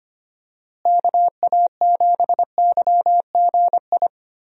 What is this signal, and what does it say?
Morse code, heard as a keyed tone, about 25 wpm, 700 Hz. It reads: KA7YGI
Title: USA Call Signs - 6 Characters - Repeated 3x - 25wpm